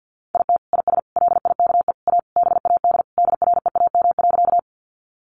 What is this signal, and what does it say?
Morse code, heard as a keyed tone, about 50 wpm, 700 Hz. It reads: IT IS LIKE A BAD DREAM.
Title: Sentences from Top 1500 Words 50wpm